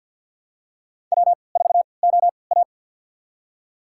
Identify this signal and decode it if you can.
Morse code, heard as a keyed tone, about 50 wpm, spaced as for 28 wpm, 700 Hz. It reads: W4OA
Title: USA Call Signs - 4 Characters - 3x Character Spacing - 50wpm